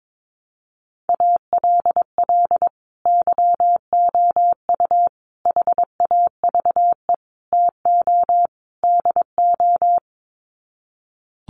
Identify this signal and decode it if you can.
Morse code, heard as a keyed tone, about 22 wpm, 700 Hz. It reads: ALL YOU HAVE TO DO
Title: Sets of 5 Words 22wpm